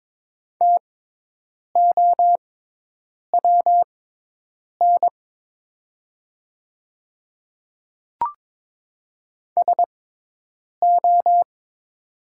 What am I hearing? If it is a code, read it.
Morse code, heard as a keyed tone, about 22 wpm, spaced as for 8 wpm, 700 Hz. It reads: TOWN SO
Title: 6x Character Spacing - Top 500 Words - 22wpm